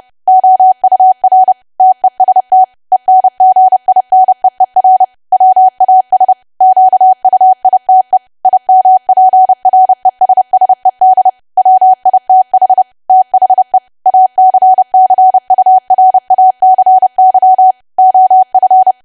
Morse code, heard as a keyed tone, about 30 wpm, 750 Hz. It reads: OUR TEST ENGINEER WAS QUITE IMPRESSED WITH THE ACCURACY OF
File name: cwa / W1AW-Level3 / 30WPM